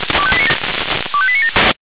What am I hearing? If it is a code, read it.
Five-tone selective call (ZVEI), heard as quick runs of five tones, tones of about 70 ms. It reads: 25897